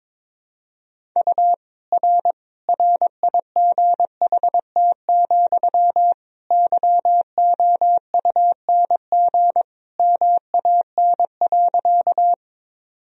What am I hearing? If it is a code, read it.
Morse code, heard as a keyed tone, about 22 wpm, 700 Hz. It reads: U R RIGHT, YOUNG MAN.
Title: Sentences from Top 400 Words and Abbreviations 22wpm